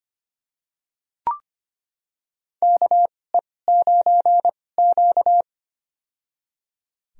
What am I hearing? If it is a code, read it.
Morse code, heard as a keyed tone, about 25 wpm, spaced as for 18 wpm, 700 Hz. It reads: KE9Q
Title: USA Call Signs - 4 Characters - 2x Character Spacing - 25wpm